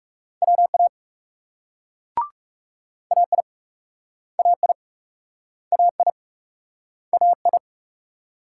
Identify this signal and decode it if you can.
Morse code, heard as a keyed tone, about 38 wpm, 700 Hz. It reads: WA AI AI AI AI